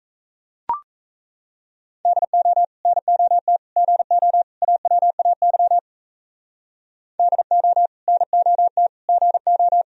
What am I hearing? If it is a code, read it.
Morse code, heard as a keyed tone, about 34 wpm, 700 Hz. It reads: DO NOT GO AWAY DO NOT GO
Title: Speed-Racing - Sets of 4 Words 28wpm